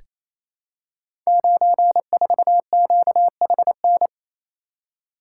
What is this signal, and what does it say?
Morse code, heard as a keyed tone, about 28 wpm, 700 Hz. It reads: 94QHN
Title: Code Groups - 5 Letter:Numbers - 28wpm